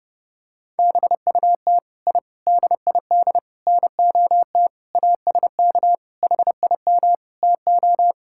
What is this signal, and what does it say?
Morse code, heard as a keyed tone, about 32 wpm, 700 Hz. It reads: BUT I DID NOT ASK HIM TO
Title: Sentences from Top 1000 Words - Review Part 5 - 30wpm